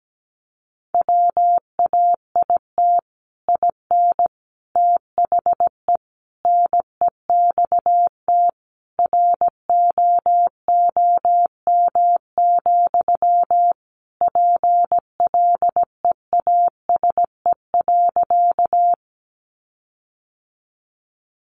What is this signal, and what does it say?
Morse code, heard as a keyed tone, about 17 wpm, 700 Hz. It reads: WAIT IN THE NEXT ROOM, PLEASE.
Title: Sentences from Top 1500 Words 17wpm